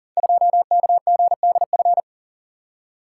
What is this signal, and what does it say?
Morse code, heard as a keyed tone, about 39 wpm, 690 Hz. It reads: 2KGDF
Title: Code Groups - 5 Letter:Numbers - 40wpm